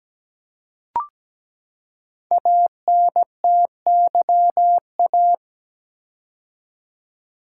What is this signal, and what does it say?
Morse code, heard as a keyed tone, about 17 wpm, 700 Hz. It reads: ANTYA
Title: Code Groups - 5 Letter:Numbers - 17wpm